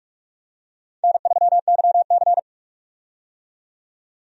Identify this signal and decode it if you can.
Morse code, heard as a keyed tone, about 43 wpm, 700 Hz. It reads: N3YC
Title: USA Call Signs - 4 Characters - 45wpm